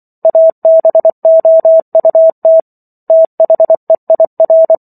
Morse code, heard as a keyed tone, about 24 wpm, 650 Hz. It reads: ABOUT THEIR